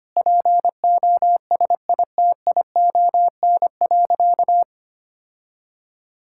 Morse code, heard as a keyed tone, about 25 wpm, 700 Hz. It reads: POSITION.
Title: Sentences from Top 1000 Words - Review Part 5 - 25wpm